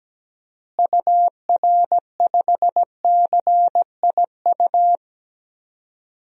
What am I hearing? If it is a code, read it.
Morse code, heard as a keyed tone, about 17 wpm, 700 Hz. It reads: UR5CIU
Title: DXCC-100 Call Signs - 6 Characters - Repeated 3x - 17wpm